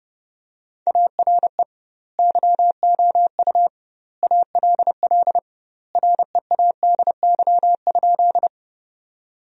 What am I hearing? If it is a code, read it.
Morse code, heard as a keyed tone, about 30 wpm, 700 Hz. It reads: ARE YOU ALL READY?
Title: 2x Word-Spacing - Sentences from Top 400 Words - 30wpm